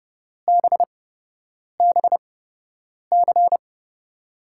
Morse code, heard as a keyed tone, about 30 wpm, spaced as for 8 wpm, 700 Hz. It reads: BBC
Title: USA Call Signs - 5 Characters - 8x Character Spacing - 30wpm